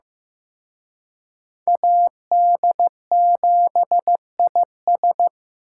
Morse code, heard as a keyed tone, about 18 wpm, 700 Hz. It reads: AD7IS
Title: Speed Racing - USA 2x2 Call Signs 15wpm